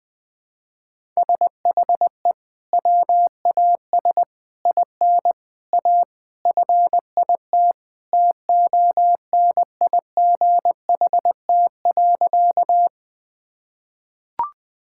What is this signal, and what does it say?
Morse code, heard as a keyed tone, about 20 wpm, 700 Hz. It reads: SHE WAS IN A FIT TONIGHT.
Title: Sentences from Top 1000 Words - Review Part 5 - 20wpm